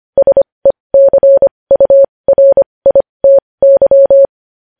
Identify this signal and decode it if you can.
Morse code, heard as a keyed tone, about 25 wpm, spaced as for 20 wpm, 550 Hz. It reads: SECURITY
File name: podcast / News headlines-1x-25x20